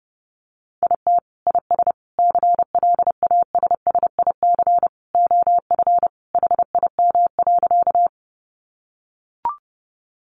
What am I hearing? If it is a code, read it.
Morse code, heard as a keyed tone, about 30 wpm, 700 Hz. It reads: IT IS CLASSIC OF HIM.
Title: Sentences from Top 1000 Words 30wpm